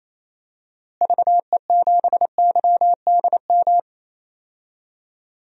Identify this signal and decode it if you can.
Morse code, heard as a keyed tone, about 27 wpm, 700 Hz. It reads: VE7YDM
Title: DXCC-100 Call Signs - 6 Characters - Repeated 2x - 28wpm